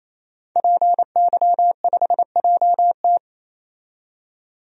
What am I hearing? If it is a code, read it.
Morse code, heard as a keyed tone, about 28 wpm, 700 Hz. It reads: PY5JT
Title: DXCC-100 Call Signs - 5 Characters - 28wpm